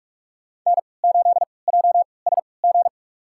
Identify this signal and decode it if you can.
Morse code, heard as a keyed tone, about 46 wpm, spaced as for 26 wpm, 700 Hz. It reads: N8JSG
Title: USA Call Signs - 5 Characters - 3x Character Spacing - 45wpm